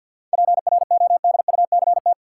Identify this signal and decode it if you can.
Morse code, heard as a keyed tone, about 47 wpm, 700 Hz. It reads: PRODUCT
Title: Sentences from Top 1000 Words - Review Part 2 - New Word Drill - 50wpm